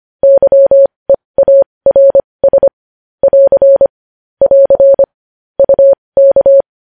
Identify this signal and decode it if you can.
Morse code, heard as a keyed tone, about 25 wpm, spaced as for 20 wpm, 550 Hz. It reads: YEARS <AR> <AR> UK
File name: podcast / News headlines-1x-25x20